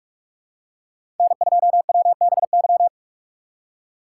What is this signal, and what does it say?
Morse code, heard as a keyed tone, about 47 wpm, 700 Hz. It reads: N2WBY
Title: USA Call Signs - 5 Characters - 2x Character Spacing - 45wpm